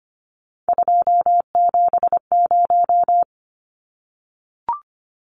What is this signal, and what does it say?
Morse code, heard as a keyed tone, about 25 wpm, 700 Hz. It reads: 270
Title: Numbers - 3-Digit 25wpm